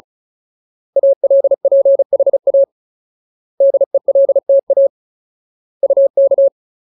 Morse code, heard as a keyed tone, about 34 wpm, 550 Hz. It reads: ALPHA DELTA UK
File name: Ultimate Word List / 35WPM Ultimate